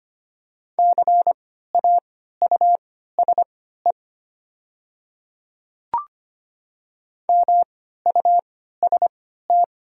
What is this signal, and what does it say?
Morse code, heard as a keyed tone, about 25 wpm, spaced as for 14 wpm, 700 Hz. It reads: CAUSE MUST
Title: 3x Character Spacing - Top 200 Words - 25wpm